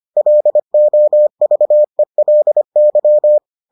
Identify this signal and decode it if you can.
Morse code, heard as a keyed tone, about 25 wpm, 600 Hz. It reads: LOVELY